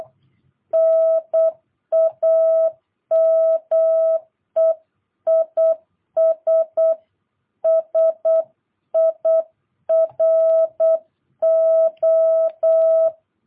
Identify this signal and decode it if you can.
Morse code, heard as a keyed tone, about 8 wpm, 650 Hz. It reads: NAME IS SIRO
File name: jh8cbh / 2022cw